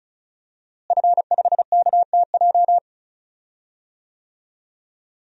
Morse code, heard as a keyed tone, about 36 wpm, 700 Hz. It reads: F5KTJ
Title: DXCC-100 Call Signs - 5 Characters - Repeated 3x - 35wpm